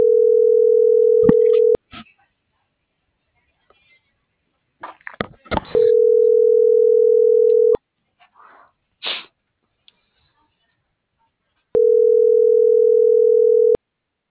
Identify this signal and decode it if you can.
Ringback tone (North America), 440 + 480 Hz.